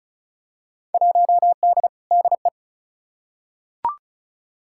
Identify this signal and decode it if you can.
Morse code, heard as a keyed tone, about 34 wpm, 700 Hz. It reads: 1D DE
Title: Field Day Exchanges - 35wpm